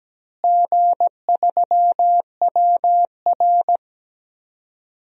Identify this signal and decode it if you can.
Morse code, heard as a keyed tone, about 17 wpm, 700 Hz. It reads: G3WR
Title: DXCC-100 Call Signs - 4 Characters - Repeated 3x - 17wpm